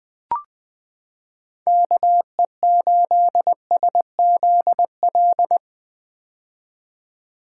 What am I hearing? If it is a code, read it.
Morse code, heard as a keyed tone, about 20 wpm, 700 Hz. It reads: KE8SZL
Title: 2023 POTA Activator Call Signs - 20wpm